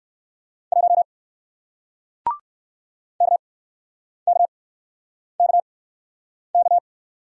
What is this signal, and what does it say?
Morse code, heard as a keyed tone, about 60 wpm, spaced as for 9 wpm, 700 Hz. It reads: 1 KKKK